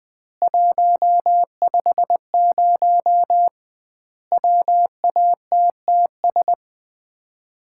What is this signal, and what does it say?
Morse code, heard as a keyed tone, about 21 wpm, 700 Hz. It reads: 150 WATTS
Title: Common QSO Phrases - 2x Word-Spacing - 20wpm